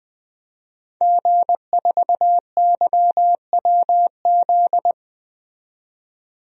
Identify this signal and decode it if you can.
Morse code, heard as a keyed tone, about 20 wpm, 700 Hz. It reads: G4YWZ
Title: DXCC-100 Call Signs - 5 Characters - 20wpm